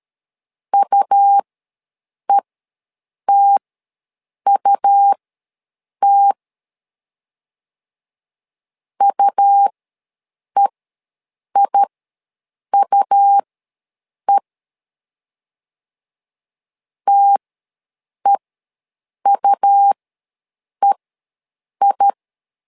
Morse code, heard as a keyed tone, about 13 wpm, spaced as for 7 wpm, 800 Hz. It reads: UETUT UEIUE TEUEI
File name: LEZIONI / 2 UHDN / 2 U ws 12wpm 1655